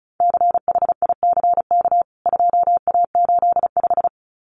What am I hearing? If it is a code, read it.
Morse code, heard as a keyed tone, about 34 wpm, 700 Hz. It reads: CHICK 2A85
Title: CWT Exchange 35wpm